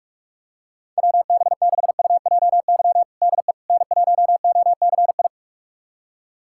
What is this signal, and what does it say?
Morse code, heard as a keyed tone, about 46 wpm, 690 Hz. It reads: WB6UJY DE N1OCI
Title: USA Callsign DE Callsign 45wpm